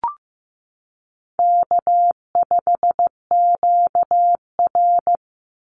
Morse code, heard as a keyed tone, about 15 wpm, 700 Hz. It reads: K5QR
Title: NA SOTA Chaser Callsigns 15wpm